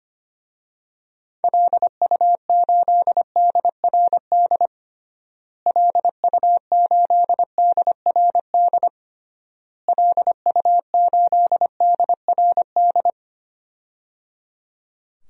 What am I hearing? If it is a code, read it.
Morse code, heard as a keyed tone, about 25 wpm, 700 Hz. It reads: LU8DRD LU8DRD LU8DRD
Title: DXCC-100 Call Signs - 6 Characters - Repeated 3x - 25wpm